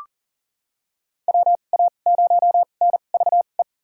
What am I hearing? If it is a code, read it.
Morse code, heard as a keyed tone, about 40 wpm, spaced as for 29 wpm, 700 Hz. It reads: WA0NVE
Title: USA Call Signs - 6 Characters - 2x Character Spacing - 40wpm